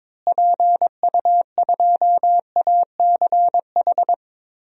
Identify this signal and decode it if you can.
Morse code, heard as a keyed tone, about 21 wpm, 700 Hz. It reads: PU2ACH
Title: DXCC-100 Call Signs - 6 Characters - Repeated 3x - 22wpm